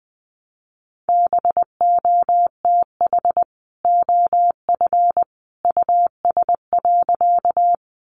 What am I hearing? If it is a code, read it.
Morse code, heard as a keyed tone, about 20 wpm, 700 Hz. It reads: BOTH OF US.
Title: Sentences from Top 300 Words 20wpm